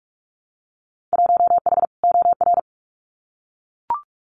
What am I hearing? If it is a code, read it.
Morse code, heard as a keyed tone, about 44 wpm, 700 Hz. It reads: 1H OR
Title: Winter Field Day Exchanges 45wpm